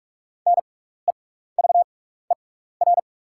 Morse code, heard as a keyed tone, about 44 wpm, spaced as for 15 wpm, 690 Hz. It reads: NEVER
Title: 6x Character Spacing - Top 500 Words - 45wpm